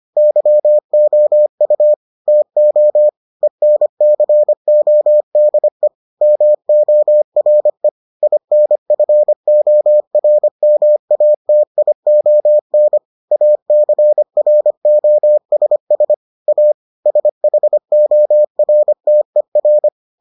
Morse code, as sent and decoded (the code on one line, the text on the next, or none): -.-- --- ..- / - --- / . -. -.-. --- -.. / -- --- .-. / .. -. ..-. --- .-. -- .- - .. --- -. / .- -.-. .-. --- ... ... / .- / ... .... --- .-. - . .-.
YOU TO ENCODE MORE INFORMATION ACROSS A SHORTER